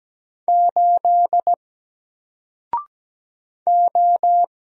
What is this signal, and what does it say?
Morse code, heard as a keyed tone, about 17 wpm, 700 Hz. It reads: 8 O
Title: Single Character (Letter:Number:Punctuation:Prosign) 17wpm